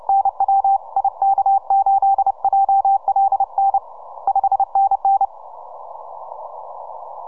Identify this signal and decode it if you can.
Morse code, heard as a keyed tone, about 30 wpm, 780 Hz. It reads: NW IK8JLN 5NN
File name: simple-file-list / H40GC 15MCW